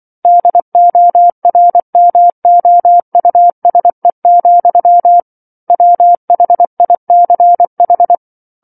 Morse code, heard as a keyed tone, about 24 wpm, 700 Hz. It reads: DORMOUSE, WHICH
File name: 07-Chapter-07 24 WPM